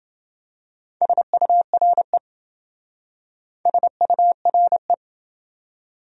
Morse code, heard as a keyed tone, about 27 wpm, 700 Hz. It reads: SURE SURE